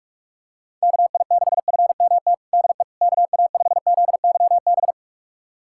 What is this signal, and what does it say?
Morse code, heard as a keyed tone, about 46 wpm, 690 Hz. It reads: KI6FMT DE KA5ZYB
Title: USA Callsign DE Callsign - 6 Characters - 45wpm